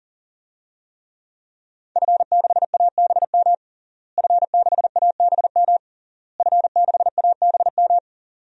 Morse code, heard as a keyed tone, about 40 wpm, 700 Hz. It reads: F6ABM F6ABM F6ABM
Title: DXCC-100 Call Signs - 5 Characters - Repeated 3x - 40wpm